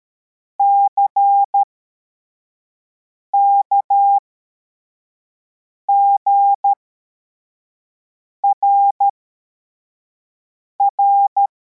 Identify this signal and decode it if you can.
Morse code, heard as a keyed tone, about 13 wpm, spaced as for 4 wpm, 800 Hz. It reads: CKGRR
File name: formation / cw / Text33